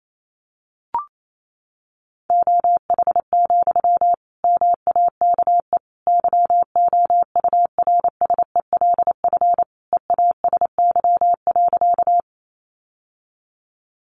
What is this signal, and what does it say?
Morse code, heard as a keyed tone, about 28 wpm, 700 Hz. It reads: OH, MAKE YOURSELF EASY.
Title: Sentences from Top 700 Words 28wpm